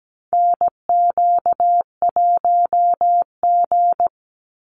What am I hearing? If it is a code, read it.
Morse code, heard as a keyed tone, about 17 wpm, 700 Hz. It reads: NQ1G
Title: USA Call Signs 17wpm